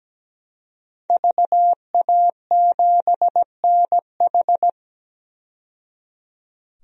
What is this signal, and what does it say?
Morse code, heard as a keyed tone, about 17 wpm, 700 Hz. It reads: VA7NH
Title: DXCC-100 Call Signs - 5 Characters - 17wpm